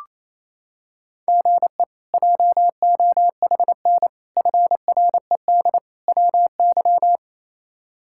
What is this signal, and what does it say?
Morse code, heard as a keyed tone, about 27 wpm, 700 Hz. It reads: GE JOHN FRED WY
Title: K1USN SST QSO Practice - 28wpm